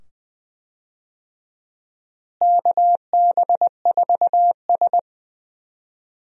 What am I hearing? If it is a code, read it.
Morse code, heard as a keyed tone, about 20 wpm, 700 Hz. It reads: KB4S
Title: USA Call Signs - 4 Characters - Repeated 2x - 20wpm